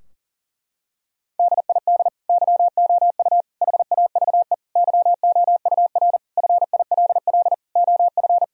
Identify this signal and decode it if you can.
Morse code, heard as a keyed tone, about 40 wpm, 700 Hz. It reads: DID YOU HAVE YOUR FILL OF
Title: Sentences from Top 1000 Words - Review Part 3 - 40wpm